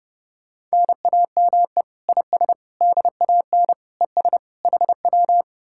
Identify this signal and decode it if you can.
Morse code, heard as a keyed tone, about 40 wpm, 700 Hz. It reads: NAME IS DAN ES HW